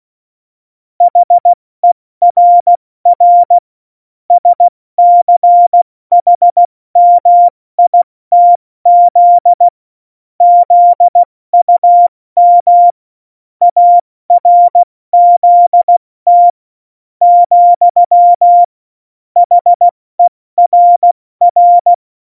Morse code, heard as a keyed tone, about 16 wpm, spaced as for 14 wpm, 700 Hz. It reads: HERR SCHMITZ ZUM ARZT, HERR
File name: Morsekurs / Witze / Witz 05 Mai, 2020 70 F80